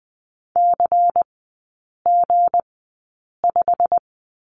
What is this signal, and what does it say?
Morse code, heard as a keyed tone, about 20 wpm, spaced as for 8 wpm, 700 Hz. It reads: CG5
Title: 159 Lesson 26e - Letter g - Three Characters 20wpm